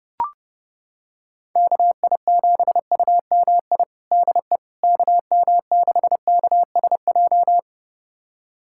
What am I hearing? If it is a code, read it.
Morse code, heard as a keyed tone, about 30 wpm, 700 Hz. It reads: KI7UMI DE KM6KSJ
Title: USA Callsign DE Callsign - 6 Characters - 30wpm